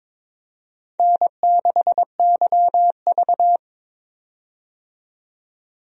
Morse code, heard as a keyed tone, about 22 wpm, 700 Hz. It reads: N6YV
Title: USA Call Signs - 4 Characters - 22wpm